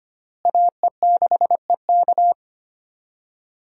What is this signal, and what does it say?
Morse code, heard as a keyed tone, about 25 wpm, 700 Hz. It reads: AE6EK
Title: USA Call Signs - 5 Characters - Repeated 3x - 25wpm